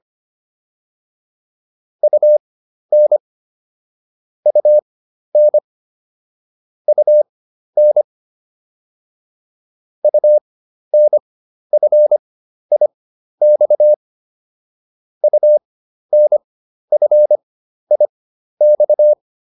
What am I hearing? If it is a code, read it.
Morse code, heard as a keyed tone, about 25 wpm, spaced as for 12 wpm, 600 Hz. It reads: UN UN UN UNFIX UNFIX